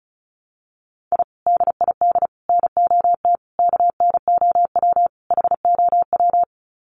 Morse code, heard as a keyed tone, about 34 wpm, 700 Hz. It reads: I DID NOT KNOW HOW
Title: Sets of 5 Words Encore 35wpm